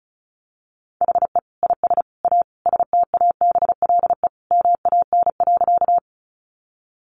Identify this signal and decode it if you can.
Morse code, heard as a keyed tone, about 34 wpm, 700 Hz. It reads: HE IS A STABLE MAN.
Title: Sentences from Top 1500 Words 35wpm